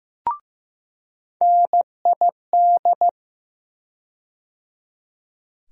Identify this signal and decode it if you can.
Morse code, heard as a keyed tone, about 15 wpm, 700 Hz. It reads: NID
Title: USA Call Sign Postfixes - 3 Letters 15wpm